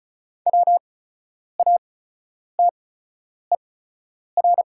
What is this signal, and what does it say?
Morse code, heard as a keyed tone, about 33 wpm, spaced as for 9 wpm, 690 Hz. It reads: WATER
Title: 8x Character Spacing - Top 300 Words - 35wpm